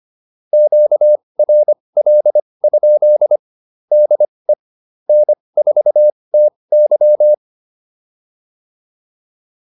Morse code, heard as a keyed tone, about 25 wpm, spaced as for 20 wpm, 600 Hz. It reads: QRL? DE N4TY
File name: QSO109a 20